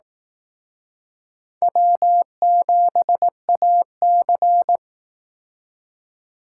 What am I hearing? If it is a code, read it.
Morse code, heard as a keyed tone, about 18 wpm, 700 Hz. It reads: W7AC
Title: Speed Racing - USA 1x2 Call Signs 15wpm